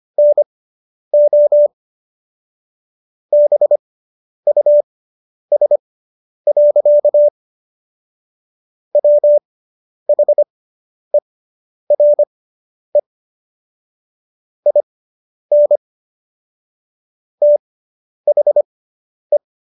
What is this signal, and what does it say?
Morse code, heard as a keyed tone, about 25 wpm, spaced as for 10 wpm, 590 Hz. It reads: NO BUS. WHERE IN THE